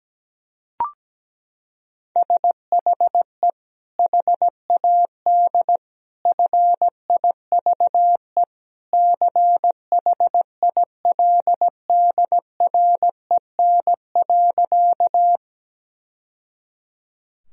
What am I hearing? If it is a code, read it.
Morse code, heard as a keyed tone, about 17 wpm, 700 Hz. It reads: SHE HAD FIVE CHILDREN.
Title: Sentences from Top 1000 Words - Review Part 2 - 17wpm